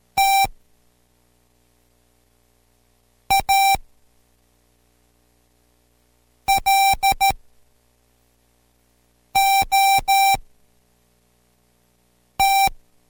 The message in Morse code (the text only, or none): T A LOT